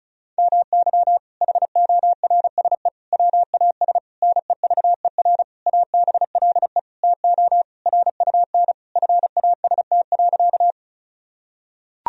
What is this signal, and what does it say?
Morse code, heard as a keyed tone, about 34 wpm, 700 Hz. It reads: MY HORSE WAS NEVER ABLE TO RUN FAST.
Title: Sentences from Top 1000 Words - Review Part 2 - 35wpm